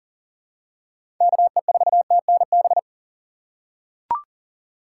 Morse code, heard as a keyed tone, about 40 wpm, 700 Hz. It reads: KE4TNB
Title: USA Call Signs - 2x Character Spacing - 40wpm